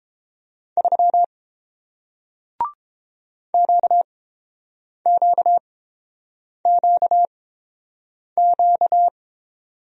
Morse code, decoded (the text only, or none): 3 QQQQ